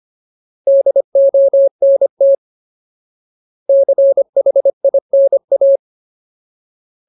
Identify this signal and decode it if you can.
Morse code, heard as a keyed tone, about 25 wpm, 550 Hz. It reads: DONT CHINA
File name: Ultimate Word List / 25WPM Ultimate